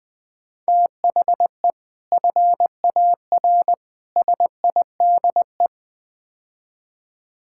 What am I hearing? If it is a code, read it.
Morse code, heard as a keyed tone, about 20 wpm, 700 Hz. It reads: THE FAR SIDE
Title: Build A Phrase - 5-Word Phrases - 20wpm